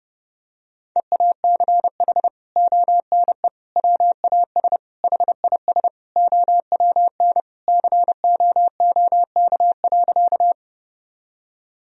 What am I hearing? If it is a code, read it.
Morse code, heard as a keyed tone, about 30 wpm, 700 Hz. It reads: EACH ONE WAS HIS OWN COOK.
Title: Sentences from Top 1000 Words - Review Part 4 - 30wpm